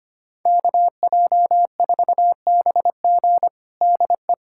Morse code, heard as a keyed tone, about 25 wpm, 700 Hz. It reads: KJ4BG DE
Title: USA Callsign DE Callsign 25wpm